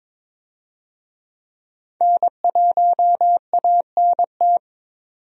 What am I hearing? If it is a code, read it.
Morse code, heard as a keyed tone, about 22 wpm, 700 Hz. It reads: N1ANT
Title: USA Call Signs - 5 Characters - 22wpm